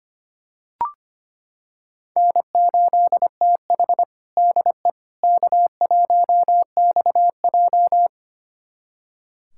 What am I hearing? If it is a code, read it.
Morse code, heard as a keyed tone, about 25 wpm, 700 Hz. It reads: N8TH DE K1XJ
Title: USA Callsign DE Callsign - 4 Characters - 25wpm